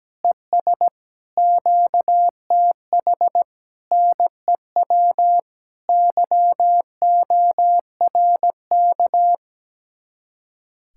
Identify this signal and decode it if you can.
Morse code, heard as a keyed tone, about 17 wpm, 700 Hz. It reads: ES QTH NEW YORK
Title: Speed-Racing Common QSO Phrases - 17wpm